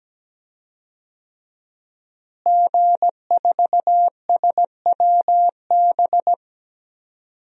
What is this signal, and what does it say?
Morse code, heard as a keyed tone, about 17 wpm, 700 Hz. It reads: G4SWB
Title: DXCC-100 Call Signs - 5 Characters - Repeated 2x - 17wpm